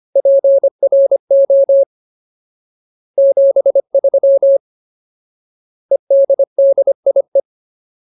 Morse code, heard as a keyed tone, about 25 wpm, 550 Hz. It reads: PRO 73 EDDIE
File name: Ultimate Word List / 25WPM Ultimate